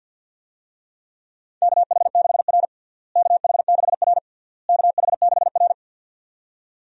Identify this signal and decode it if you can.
Morse code, heard as a keyed tone, about 53 wpm, 700 Hz. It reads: KH6R KH6R KH6R
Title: USA Call Signs - 4 Characters - Repeated 3x - 50wpm